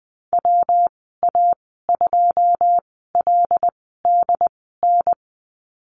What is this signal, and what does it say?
Morse code, heard as a keyed tone, about 20 wpm, spaced as for 14 wpm, 700 Hz. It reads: WA2LDN
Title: 140 Lesson 23g - Letter f - Call Signs 20wpm